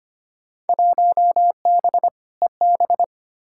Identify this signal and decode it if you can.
Morse code, heard as a keyed tone, about 25 wpm, 700 Hz. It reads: 1B EB
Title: Field Day Exchanges Encore - 25wpm